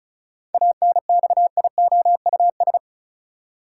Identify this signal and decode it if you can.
Morse code, heard as a keyed tone, about 34 wpm, 700 Hz. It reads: ANXIOUS
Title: Seven Letter Words - 35wpm